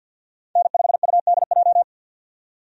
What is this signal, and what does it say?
Morse code, heard as a keyed tone, about 50 wpm, 700 Hz. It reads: N5UDJ
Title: USA Call Signs - 5 Characters - 50wpm